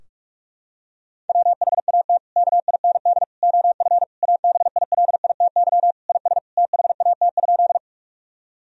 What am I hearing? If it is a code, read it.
Morse code, heard as a keyed tone, about 41 wpm, 700 Hz. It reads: WHAT KIND OF ABILITY IS THAT?
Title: Sentences from Top 1000 Words - Review Part 3 - 45wpm